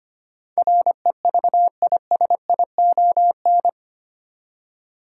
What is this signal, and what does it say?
Morse code, heard as a keyed tone, about 25 wpm, 700 Hz. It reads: REVISION
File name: Eight Letter Words - 25wpm